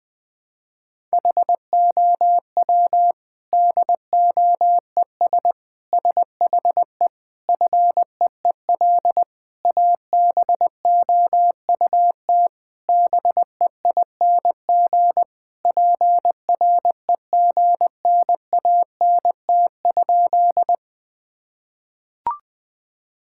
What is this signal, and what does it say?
Morse code, heard as a keyed tone, about 20 wpm, 700 Hz. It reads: HOW DOES SHE FEEL ABOUT BEING PREGNANT?
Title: Sentences from Top 1000 Words - Review Part 6 - 20wpm